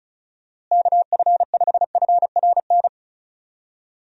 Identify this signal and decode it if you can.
Morse code, heard as a keyed tone, about 36 wpm, 700 Hz. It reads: KF5FRN
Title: USA Call Signs - 6 Characters - Repeated 2x - 35wpm